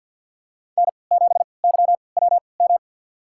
Morse code, heard as a keyed tone, about 51 wpm, spaced as for 28 wpm, 700 Hz. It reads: N7YWM
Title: USA Call Signs - 3x Character Spacing - 50wpm